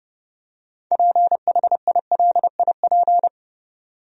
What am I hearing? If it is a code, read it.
Morse code, heard as a keyed tone, about 32 wpm, 700 Hz. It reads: PHILIP
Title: USA Common Names - 30wpm